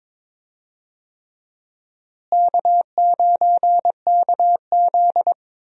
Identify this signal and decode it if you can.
Morse code, heard as a keyed tone, about 22 wpm, 700 Hz. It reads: K9KZ
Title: USA Call Signs - 4 Characters - Repeated 3x - 22wpm